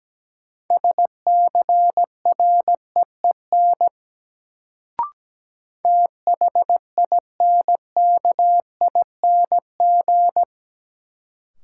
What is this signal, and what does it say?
Morse code, heard as a keyed tone, about 17 wpm, 700 Hz. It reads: SCREEN THINKING
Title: Sentences from Top 1000 Words - Review Part 6 - New Word Drill - 17wpm